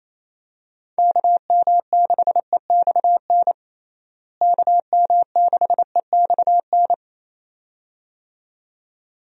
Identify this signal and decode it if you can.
Morse code, heard as a keyed tone, about 27 wpm, 700 Hz. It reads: KM6EXN KM6EXN
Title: USA Call Signs - 6 Characters - Repeated 2x - 28wpm